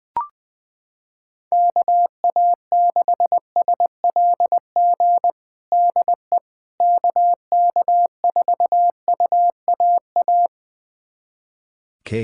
Morse code, heard as a keyed tone, about 20 wpm, 700 Hz. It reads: KA6SLG DE KK4UAA
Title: USA Callsign DE Callsign - 6 Characters - 20wpm